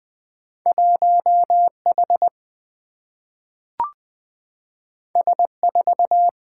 Morse code, heard as a keyed tone, about 20 wpm, 700 Hz. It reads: 1 H S 4
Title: Single-Letter Single-Number 20wpm